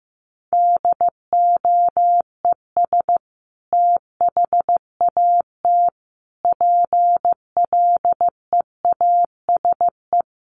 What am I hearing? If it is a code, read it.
Morse code, heard as a keyed tone, about 15 wpm, 700 Hz. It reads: DOES THAT PLEASE
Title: Sentences from Top 1500 Words 15wpm